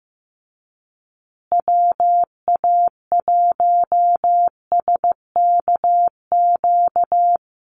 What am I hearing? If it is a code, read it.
Morse code, heard as a keyed tone, about 15 wpm, 700 Hz. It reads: WA1SKQ
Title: Call Signs Repeated 2x - 15wpm